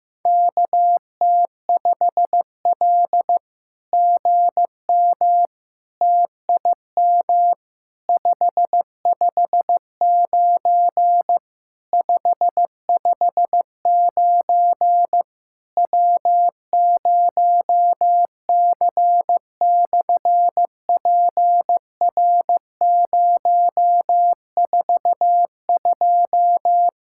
Morse code, heard as a keyed tone, about 15 wpm, 700 Hz. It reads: KT5L GM TIM 559 559 W0C/PR042
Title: SOTA QSO Practice 15wpm